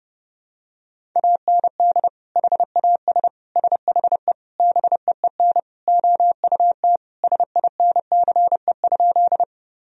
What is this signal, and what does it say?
Morse code, heard as a keyed tone, about 32 wpm, 700 Hz. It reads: AND HAS SHE BEEN OUT SINCE?
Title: Sentences from Top 1000 Words - Review Part 1 - 30wpm